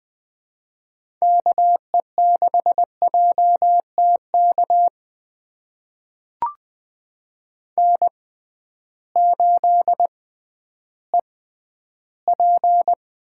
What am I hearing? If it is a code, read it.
Morse code, heard as a keyed tone, about 20 wpm, spaced as for 8 wpm, 700 Hz. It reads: KE6JT KN8EP
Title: USA Call Signs - 6x Character Spacing - 20wpm